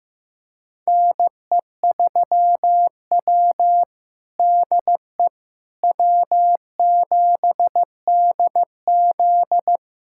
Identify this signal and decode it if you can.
Morse code, heard as a keyed tone, about 15 wpm, 700 Hz. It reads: NE3W DE W7DZ
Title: USA Callsign DE Callsign - 4 Characters - 15wpm